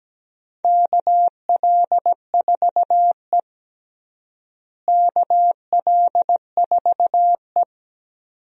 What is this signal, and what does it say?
Morse code, heard as a keyed tone, about 17 wpm, 700 Hz. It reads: KL4E KL4E
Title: USA Call Signs - 4 Characters - Repeated 2x - 17wpm